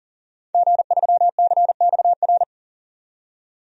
Morse code, heard as a keyed tone, about 40 wpm, 700 Hz. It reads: G3CXR
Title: DXCC-100 Call Signs - 5 Characters - 40wpm